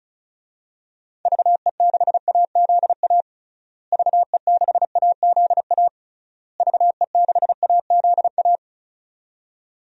Morse code, heard as a keyed tone, about 34 wpm, 700 Hz. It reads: VE6AZA VE6AZA VE6AZA
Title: DXCC-100 Call Signs - 6 Characters - Repeated 3x - 35wpm